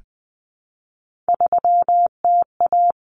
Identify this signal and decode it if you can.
Morse code, heard as a keyed tone, about 20 wpm, 700 Hz. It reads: 3TA
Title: Common Cut Numbers - 3-Digit 20wpm